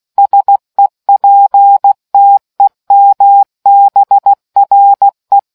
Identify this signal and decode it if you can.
Morse code, heard as a keyed tone, about 16 wpm, 800 Hz. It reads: SEPTEMBRE